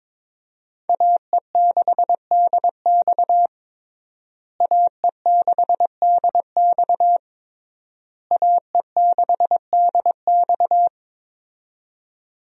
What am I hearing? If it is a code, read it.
Morse code, heard as a keyed tone, about 21 wpm, 700 Hz. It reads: AE6DX AE6DX AE6DX
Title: USA Call Signs - 5 Characters - Repeated 3x - 22wpm